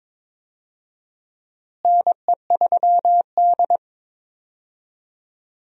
Morse code, heard as a keyed tone, about 22 wpm, 700 Hz. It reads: NE3D
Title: USA Call Signs - 4 Characters - Repeated 2x - 22wpm